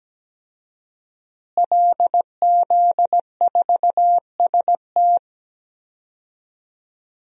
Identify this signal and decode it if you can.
Morse code, heard as a keyed tone, about 17 wpm, 700 Hz. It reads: LZ4ST
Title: DXCC-100 Call Signs - 5 Characters - Repeated 2x - 17wpm